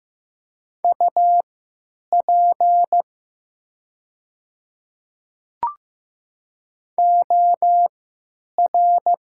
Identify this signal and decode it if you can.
Morse code, heard as a keyed tone, about 15 wpm, spaced as for 9 wpm, 700 Hz. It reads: UP OR
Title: 3x Character Spacing - Top 200 Words - 15wpm